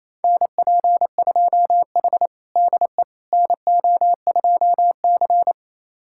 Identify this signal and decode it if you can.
Morse code, heard as a keyed tone, about 28 wpm, 700 Hz. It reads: NP2H DE NO2C
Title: USA Callsign DE Callsign - 4 Characters - 28wpm